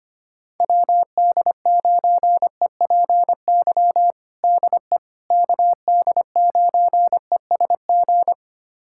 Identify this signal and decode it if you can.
Morse code, heard as a keyed tone, about 25 wpm, 700 Hz. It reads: WD9EPY DE KD9ESG
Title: USA Callsign DE Callsign 25wpm